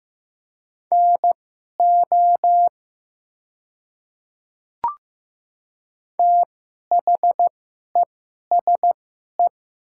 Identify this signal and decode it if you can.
Morse code, heard as a keyed tone, about 15 wpm, spaced as for 11 wpm, 690 Hz. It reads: NO THESE
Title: 2x Character Spacing - Top 100 Words - 15wpm